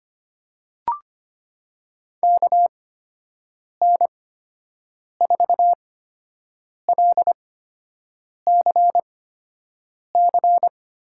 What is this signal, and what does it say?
Morse code, heard as a keyed tone, about 25 wpm, spaced as for 7 wpm, 700 Hz. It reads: KN4LCC
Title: USA Call Signs - 6 Characters - 8x Character Spacing - 25wpm